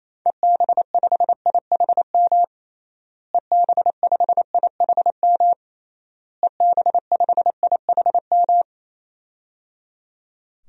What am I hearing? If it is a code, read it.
Morse code, heard as a keyed tone, about 28 wpm, 700 Hz. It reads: EB5IHM EB5IHM EB5IHM
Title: DXCC-100 Call Signs - 6 Characters - Repeated 3x - 28wpm